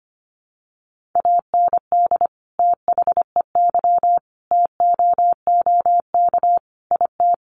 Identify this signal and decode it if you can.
Morse code, heard as a keyed tone, about 25 wpm, 700 Hz. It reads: AND THEY TOOK IT
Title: Sentences from Top 1400 Words 25wpm